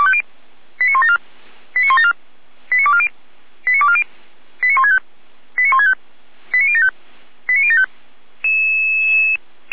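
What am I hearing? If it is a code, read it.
Five-tone selective call (ZVEI), heard as quick runs of five tones, tones of about 70 ms. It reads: 89174 89174 89240 89240 89165 89165 89075 89075
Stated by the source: ZVEI five-tone call received by SDR, logged by fire brigade Feuerwehr Orlamünde (f85360).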